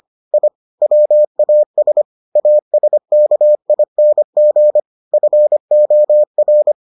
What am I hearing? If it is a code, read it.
Morse code, heard as a keyed tone, about 25 wpm, 600 Hz. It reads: I WAS ASKING FOR